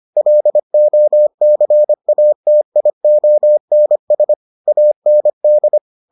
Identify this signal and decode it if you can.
Morse code, heard as a keyed tone, about 25 wpm, 600 Hz. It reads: LOCATIONS AND